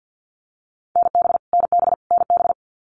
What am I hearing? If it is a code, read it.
Morse code, heard as a keyed tone, about 50 wpm, 700 Hz. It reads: NB NB NB
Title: Canadian Provinces and Territories - Repeated 3x - 50wpm